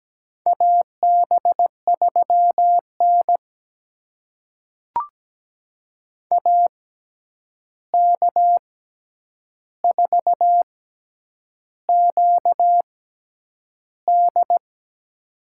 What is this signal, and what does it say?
Morse code, heard as a keyed tone, about 17 wpm, spaced as for 8 wpm, 700 Hz. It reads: AB3N AK4QD
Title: USA Call Signs - 6x Character Spacing - 17wpm